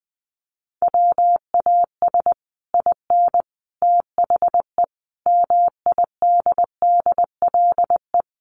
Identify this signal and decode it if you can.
Morse code, heard as a keyed tone, about 20 wpm, 700 Hz. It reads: WAS IN THE MIDDLE
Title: Sets of 5 Words Encore 20wpm